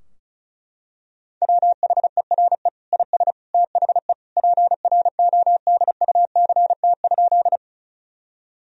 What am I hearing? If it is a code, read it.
Morse code, heard as a keyed tone, about 36 wpm, 700 Hz. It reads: WHERE IS THE PRODUCT?
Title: Sentences from Top 1000 Words - Review Part 2 - 35wpm